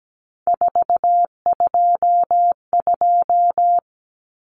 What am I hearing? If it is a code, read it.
Morse code, heard as a keyed tone, about 17 wpm, 700 Hz. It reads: 422
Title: Numbers - 3-Digit 17wpm